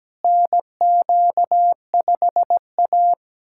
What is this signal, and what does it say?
Morse code, heard as a keyed tone, about 17 wpm, 700 Hz. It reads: NQ5A
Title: USA Callsign DE Callsign - 4 Characters - 17wpm